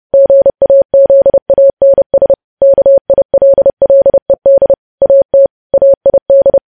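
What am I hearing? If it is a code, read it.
Morse code, heard as a keyed tone, about 30 wpm, 550 Hz. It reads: GAZANS KILLED AT AID
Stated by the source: Morse code news headlines sent at 30 words per minute.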